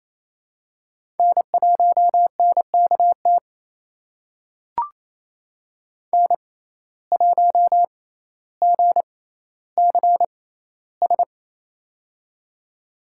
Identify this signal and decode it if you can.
Morse code, heard as a keyed tone, about 28 wpm, spaced as for 14 wpm, 700 Hz. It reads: N1NKT N1GCS
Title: USA Call Signs - 5 Characters - 6x Character Spacing - 28wpm